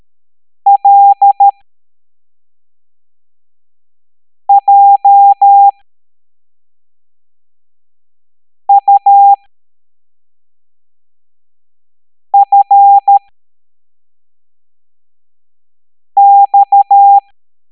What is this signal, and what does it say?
Morse code, heard as a keyed tone, about 13 wpm, spaced as for 3 wpm, 800 Hz. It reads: LJUFX
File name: hamdata / L09-EX04